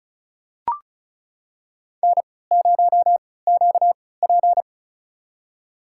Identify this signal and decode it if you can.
Morse code, heard as a keyed tone, about 36 wpm, spaced as for 20 wpm, 700 Hz. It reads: N0QP
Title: USA Call Signs - 3x Character Spacing - 35wpm